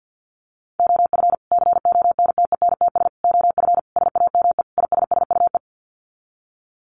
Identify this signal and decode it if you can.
Morse code, heard as a keyed tone, about 51 wpm, 700 Hz. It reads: OF CONTENTS OF SAME ISSUE
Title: Sets of 5 Words Encore2 50wpm